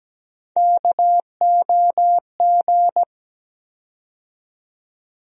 Morse code, heard as a keyed tone, about 17 wpm, 700 Hz. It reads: KOG
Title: USA Call Sign Postfixes - 3 Letters 17wpm